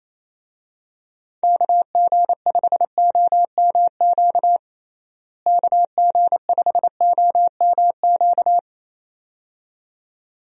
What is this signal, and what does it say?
Morse code, heard as a keyed tone, about 27 wpm, 700 Hz. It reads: KG5OMQ KG5OMQ
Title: USA Call Signs - 6 Characters - Repeated 2x - 28wpm